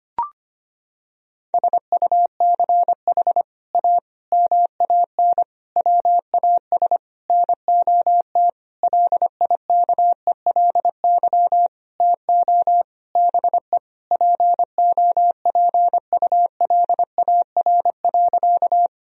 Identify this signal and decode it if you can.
Morse code, heard as a keyed tone, about 25 wpm, 700 Hz. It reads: SUCH A MAN WAS NOT LIKELY TO BE POPULAR.
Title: Sentences from Top 1000 Words - Review Part 4 - 25wpm